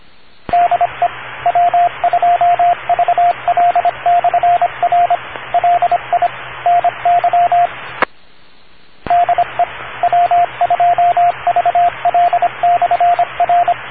Morse code, heard as a keyed tone, about 26 wpm, 700 Hz, with radio noise. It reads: DE W2VL/R LI NY DE W2VL/R